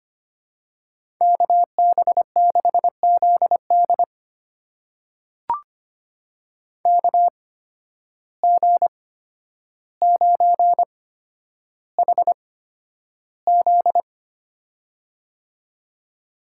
Morse code, heard as a keyed tone, about 25 wpm, spaced as for 11 wpm, 700 Hz. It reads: KB6ZD KG9HZ
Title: USA Call Signs - 5 Characters - 8x Character Spacing - 25wpm